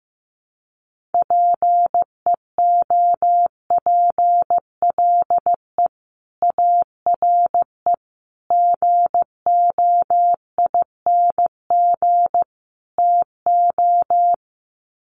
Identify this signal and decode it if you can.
Morse code, heard as a keyed tone, about 15 wpm, 700 Hz. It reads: PEOPLE ARE GOING TO
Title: Sets of 4 Words 15wpm